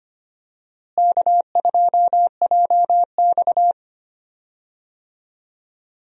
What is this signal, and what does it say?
Morse code, heard as a keyed tone, about 25 wpm, 700 Hz. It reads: K2JX
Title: USA Call Signs - 4 Characters - Repeated 2x - 25wpm